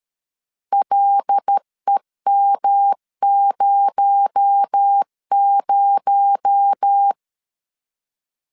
Morse code, heard as a keyed tone, about 13 wpm, 800 Hz. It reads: LEM00
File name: LEZIONI / 6 PKX0 / 6 PKX0 12wpm 2231